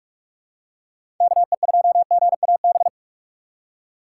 Morse code, heard as a keyed tone, about 44 wpm, 700 Hz. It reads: K E 2 G A B
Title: USA Call Signs - 6 Characters - 6x Character Spacing - 45wpm